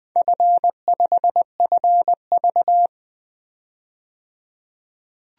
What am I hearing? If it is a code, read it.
Morse code, heard as a keyed tone, about 20 wpm, 700 Hz. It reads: F5FV
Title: DXCC-100 Call Signs - 4 Characters - 20wpm